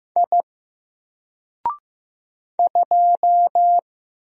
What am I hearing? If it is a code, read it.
Morse code, heard as a keyed tone, about 15 wpm, 700 Hz. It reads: I 2
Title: Single Character (Letter:Number:Punctuation:Prosign) 15wpm